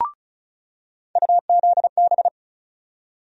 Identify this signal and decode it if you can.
Morse code, heard as a keyed tone, about 36 wpm, 700 Hz. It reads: UZB
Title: USA Call Sign Postfixes - 3 Letters 35wpm